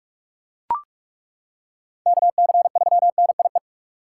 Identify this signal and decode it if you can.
Morse code, heard as a keyed tone, about 42 wpm, 700 Hz. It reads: KC3NIE
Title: K1USN SST QSO Practice - 45wpm